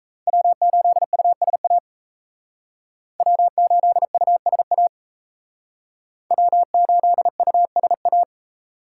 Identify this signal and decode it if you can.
Morse code, heard as a keyed tone, about 35 wpm, 700 Hz. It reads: W8USA W8USA W8USA